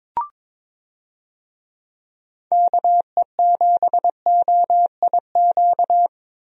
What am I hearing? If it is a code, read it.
Morse code, heard as a keyed tone, about 22 wpm, 700 Hz. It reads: KE7OIQ
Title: USA Call Signs - 6 Characters - 22wpm